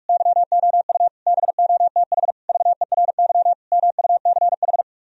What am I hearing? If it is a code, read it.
Morse code, heard as a keyed tone, about 49 wpm, 700 Hz. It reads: YOU BOTH VERY MUCH